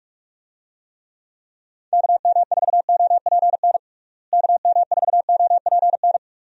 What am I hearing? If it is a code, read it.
Morse code, heard as a keyed tone, about 46 wpm, 700 Hz. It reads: KM4OPN KM4OPN
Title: USA Call Signs - 6 Characters - Repeated 2x - 45wpm